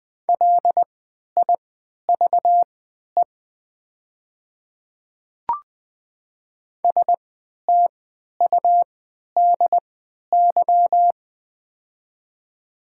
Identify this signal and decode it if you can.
Morse code, heard as a keyed tone, about 20 wpm, spaced as for 11 wpm, 700 Hz. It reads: LIVE STUDY
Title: 3x Character Spacing - Top 500 Words - 20wpm